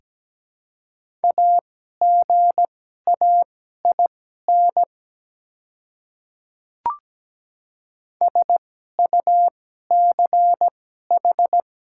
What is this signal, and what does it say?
Morse code, heard as a keyed tone, about 17 wpm, spaced as for 12 wpm, 700 Hz. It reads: AGAIN SUCH
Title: 2x Character Spacing - Top 200 Words - 17wpm